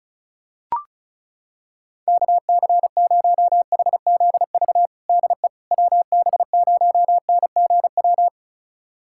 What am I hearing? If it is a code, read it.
Morse code, heard as a keyed tone, about 36 wpm, 700 Hz. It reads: KC0HZV DE WB0NGW
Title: USA Callsign DE Callsign - 6 Characters - 35wpm